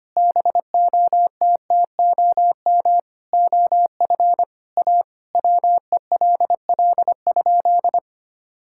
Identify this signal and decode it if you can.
Morse code, heard as a keyed tone, about 25 wpm, 700 Hz. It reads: BOTTOM OF A WELL?
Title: Sentences from Top 1000 Words - Review Part 6 - 25wpm